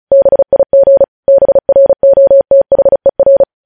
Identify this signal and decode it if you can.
Morse code, heard as a keyed tone, about 36 wpm, 550 Hz. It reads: BIG BROTHER
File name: podcast / News headlines-1x-35x35